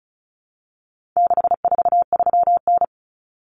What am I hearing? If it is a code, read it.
Morse code, heard as a keyed tone, about 34 wpm, 700 Hz. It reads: 643N
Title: Common Cut Numbers - 4-Digit 35wpm